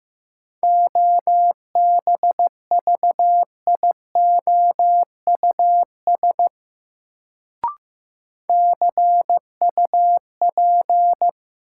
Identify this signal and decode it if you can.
Morse code, heard as a keyed tone, about 15 wpm, 700 Hz. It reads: OBVIOUS CUP
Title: Sentences from Top 1600 Words - New Word Drill Alt - 15wpm